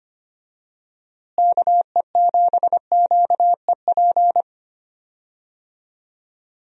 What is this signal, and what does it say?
Morse code, heard as a keyed tone, about 25 wpm, 700 Hz. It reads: KE7QEP
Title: USA Call Signs - 6 Characters - Repeated 3x - 25wpm